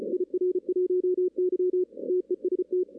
Morse code, heard as a keyed tone, about 33 wpm, 370 Hz, with radio noise. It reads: IR1Y TEST
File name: 2016 arrl dx cw / 2016-ARRL-DX-CW-K5TR- - 0-5